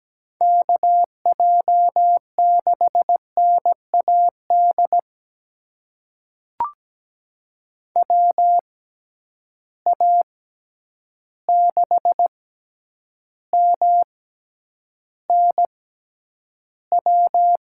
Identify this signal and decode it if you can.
Morse code, heard as a keyed tone, about 17 wpm, spaced as for 9 wpm, 700 Hz. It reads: KJ6NAD WA6MNW
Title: USA Call Signs - 6 Characters - 6x Character Spacing - 17wpm